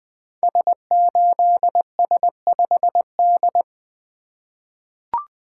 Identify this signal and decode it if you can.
Morse code, heard as a keyed tone, about 20 wpm, 700 Hz. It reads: S8S5D
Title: Code Groups - 5 Letter:Numbers - 20wpm